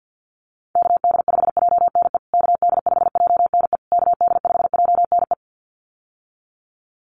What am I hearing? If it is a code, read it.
Morse code, heard as a keyed tone, about 48 wpm, 700 Hz. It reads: KD5JNE KD5JNE KD5JNE
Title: USA Call Signs Repeated 3x 50wpm